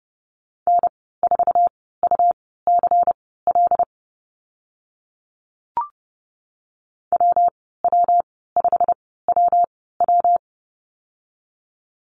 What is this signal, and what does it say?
Morse code, heard as a keyed tone, about 32 wpm, spaced as for 17 wpm, 700 Hz. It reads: N4UCL WW5WW
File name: Lesson 19g - Letter w - Call Signs 30wpm